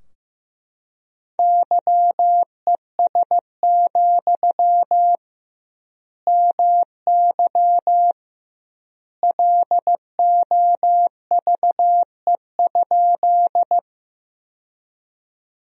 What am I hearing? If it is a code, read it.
Morse code, heard as a keyed tone, about 14 wpm, 700 Hz. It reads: YES, MY LOVE?
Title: 2x Word-Spacing - Sentences from Top 500 Words - 15wpm